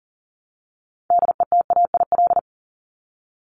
Morse code, heard as a keyed tone, about 40 wpm, 700 Hz. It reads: DETAIL
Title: Top 1251-1500 Words 40wpm